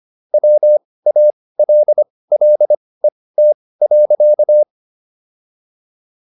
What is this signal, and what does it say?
Morse code, heard as a keyed tone, about 25 wpm, spaced as for 18 wpm, 600 Hz. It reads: WALLET.